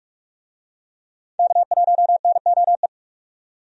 Morse code, heard as a keyed tone, about 42 wpm, 700 Hz. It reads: K1NOE
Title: USA Call Signs - 5 Characters - 45wpm